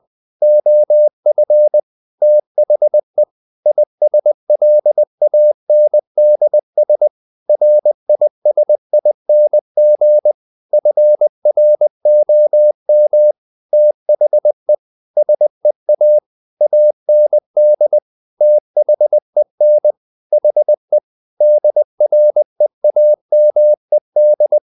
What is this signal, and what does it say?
Morse code, heard as a keyed tone, about 20 wpm, 600 Hz. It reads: OF THE ISLANDS RISING FROM THE SEA AND THEN HE DREAMED
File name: the old man and the sea / the old man and the sea part 1 20wpm